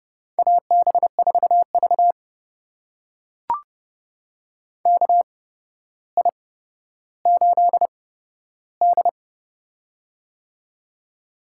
Morse code, heard as a keyed tone, about 30 wpm, spaced as for 13 wpm, 700 Hz. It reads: AB4V KI8D
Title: USA Call Signs - 4 Characters - 8x Character Spacing - 30wpm